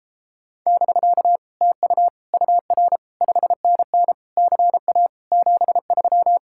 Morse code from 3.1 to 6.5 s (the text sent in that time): 5NN CA 73